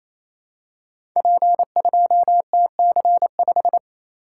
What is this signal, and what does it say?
Morse code, heard as a keyed tone, about 27 wpm, 700 Hz. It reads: P2TC5
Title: Code Groups - 5 Letter:Numbers - 28wpm